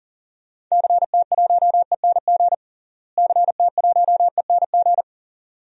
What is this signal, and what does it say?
Morse code, heard as a keyed tone, about 40 wpm, 700 Hz. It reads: CT1ENG CT1ENG
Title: DXCC-100 Call Signs - 6 Characters - Repeated 2x - 40wpm